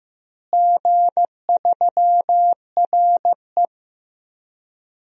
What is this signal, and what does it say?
Morse code, heard as a keyed tone, about 15 wpm, 700 Hz. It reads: G3RE
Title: DXCC-100 Call Signs - 4 Characters - Repeated 3x - 15wpm